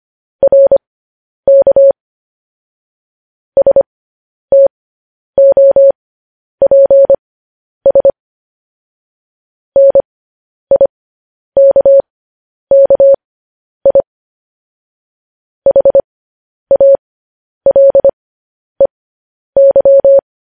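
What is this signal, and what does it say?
Morse code, heard as a keyed tone, about 25 wpm, spaced as for 10 wpm, 550 Hz. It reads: RK STOPS NIKKI HALEY
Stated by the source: Morse code news headlines practice, at 10 WPM.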